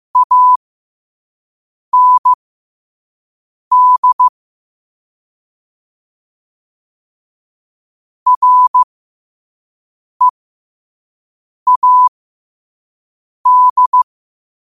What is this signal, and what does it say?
Morse code, heard as a keyed tone, about 15 wpm, spaced as for 5 wpm, 1000 Hz. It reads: AND READ
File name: qotdfiles / 5wpm